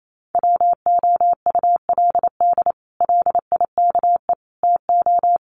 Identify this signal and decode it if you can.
Morse code, heard as a keyed tone, about 28 wpm, 700 Hz. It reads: WOULD LIKE TO
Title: Sets of 5 Words Encore2 28wpm